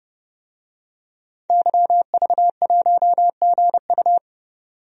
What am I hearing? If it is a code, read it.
Morse code, heard as a keyed tone, about 30 wpm, 700 Hz. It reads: YV1GU
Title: DXCC-100 Call Signs - 5 Characters - 30wpm